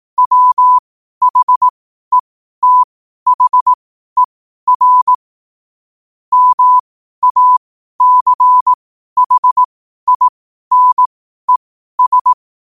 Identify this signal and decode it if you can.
Morse code, heard as a keyed tone, about 18 wpm, spaced as for 12 wpm, 1000 Hz. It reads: WHETHER MACHINES
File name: qotdfiles / 12wpm